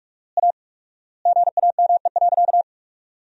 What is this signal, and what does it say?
Morse code, heard as a keyed tone, about 46 wpm, 700 Hz. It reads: A GAME.
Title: 4x Word-Spacing - Sentences from Top 500 Words - 45wpm